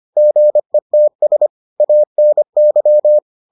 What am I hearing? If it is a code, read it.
Morse code, heard as a keyed tone, about 25 wpm, 600 Hz. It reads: GETS ANY